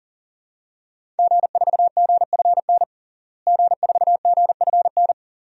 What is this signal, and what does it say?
Morse code, heard as a keyed tone, about 37 wpm, 700 Hz. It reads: G4GFN G4GFN
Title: DXCC-100 Call Signs - 5 Characters - Repeated 3x - 40wpm